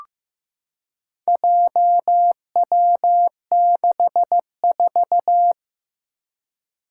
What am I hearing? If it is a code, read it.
Morse code, heard as a keyed tone, about 15 wpm, 700 Hz. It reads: JW64
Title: Four Challenging Letter-Number Combinations 15wpm